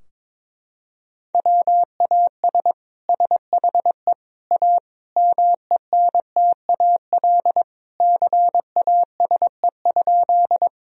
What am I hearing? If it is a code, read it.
Morse code, heard as a keyed tone, about 22 wpm, 700 Hz. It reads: WAS SHE A MENTAL CASE?
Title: Sentences from Top 1000 Words - Review Part 4 - 22wpm